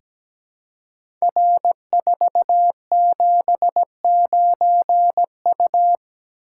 Morse code, heard as a keyed tone, about 17 wpm, 700 Hz. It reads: R479U
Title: Code Groups - 5 Letter:Numbers - 17wpm